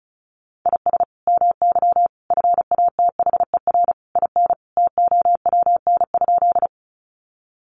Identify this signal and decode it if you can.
Morse code, heard as a keyed tone, about 34 wpm, 700 Hz. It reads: IS MY FATHER IN TOWN?
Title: Sentences from Top 500 Words 35wpm